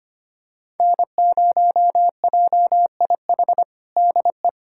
Morse code, heard as a keyed tone, about 25 wpm, 700 Hz. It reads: N0JIH DE
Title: USA Callsign DE Callsign - 5 Characters - 25wpm